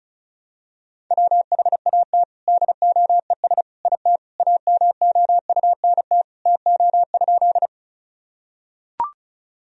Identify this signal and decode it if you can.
Morse code, heard as a keyed tone, about 36 wpm, 700 Hz. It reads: WHAT DOES IT AMOUNT TO?
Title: Sentences from Top 1000 Words - Review Part 4 - 35wpm